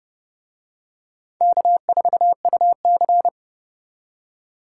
Morse code, heard as a keyed tone, about 30 wpm, 700 Hz. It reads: K4UC